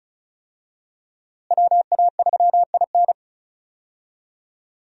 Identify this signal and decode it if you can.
Morse code, heard as a keyed tone, about 36 wpm, 700 Hz. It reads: WA3IN
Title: USA Call Signs - 5 Characters - Repeated 2x - 35wpm